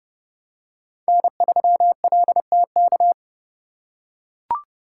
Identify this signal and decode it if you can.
Morse code, heard as a keyed tone, about 29 wpm, 700 Hz. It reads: N3LTK